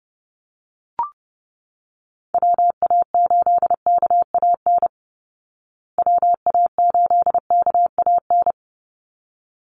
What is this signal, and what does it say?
Morse code, heard as a keyed tone, about 30 wpm, 700 Hz. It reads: WA8KAN WA8KAN
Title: Call Signs Repeated 3x - 30wpm